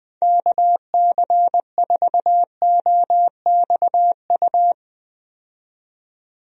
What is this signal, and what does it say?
Morse code, heard as a keyed tone, about 20 wpm, 700 Hz. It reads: KC4OXU
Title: USA Call Signs - 6 Characters - 20wpm